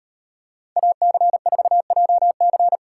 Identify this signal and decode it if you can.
Morse code, heard as a keyed tone, about 38 wpm, 700 Hz. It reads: AC4JC